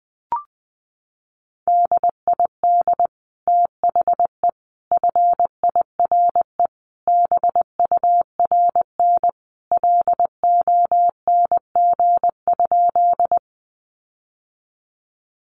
Morse code, heard as a keyed tone, about 20 wpm, 700 Hz. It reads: DID THE FIRE BURN LONG?
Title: Sentences from Top 1300 Words 20wpm